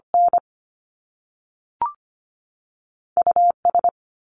Morse code, heard as a keyed tone, about 25 wpm, 700 Hz. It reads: N US
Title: Top Two Letter Combinations in Words 25wpm